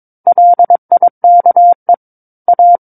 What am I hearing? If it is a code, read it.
Morse code, heard as a keyed tone, about 22 wpm, 700 Hz. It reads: LIKE A
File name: 06-Chapter-06 22 WPM